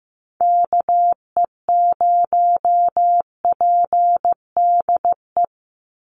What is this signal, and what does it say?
Morse code, heard as a keyed tone, about 15 wpm, 700 Hz. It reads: KE0PDE
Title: USA Call Signs 15wpm